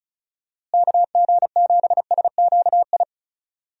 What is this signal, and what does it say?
Morse code, heard as a keyed tone, about 36 wpm, 700 Hz. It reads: KG7SQI
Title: USA Call Signs - 6 Characters - Repeated 3x - 35wpm